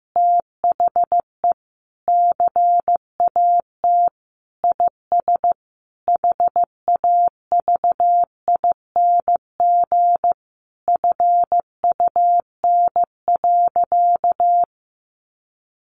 Morse code, heard as a keyed tone, about 15 wpm, 700 Hz. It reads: THE CAT IS HAVING FUN.
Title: Easy Phrases 15wpm